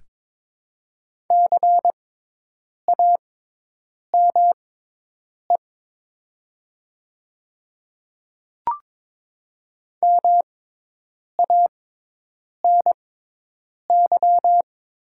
Morse code, heard as a keyed tone, about 22 wpm, spaced as for 8 wpm, 700 Hz. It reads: CAME MANY
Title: 6x Character Spacing - Top 200 Words - 22wpm